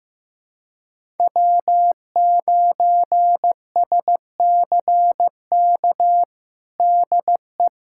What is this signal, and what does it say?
Morse code, heard as a keyed tone, about 15 wpm, 700 Hz. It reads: W9SCK DE
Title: USA Callsign DE Callsign - 5 Characters - 15wpm